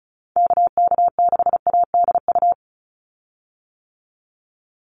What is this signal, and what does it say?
Morse code, heard as a keyed tone, about 36 wpm, 700 Hz. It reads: KK6ADU
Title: USA Call Signs 35wpm